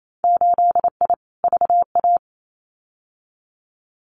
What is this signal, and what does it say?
Morse code, heard as a keyed tone, about 28 wpm, 700 Hz. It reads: 8I VA
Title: Winter Field Day Exchanges Encore 28wpm